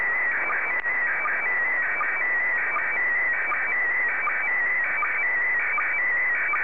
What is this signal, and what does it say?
SSTV (slow-scan television), PD 180.
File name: sstv / PD180-AX2024BLAH